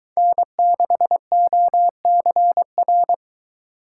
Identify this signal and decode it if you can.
Morse code, heard as a keyed tone, about 23 wpm, 700 Hz. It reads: N6OCR